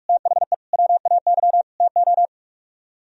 Morse code, heard as a keyed tone, about 47 wpm, 700 Hz. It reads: THE WAY TO